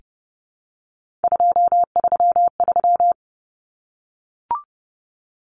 Morse code, heard as a keyed tone, about 30 wpm, 700 Hz. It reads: 233
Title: Numbers - 3-Digit 30wpm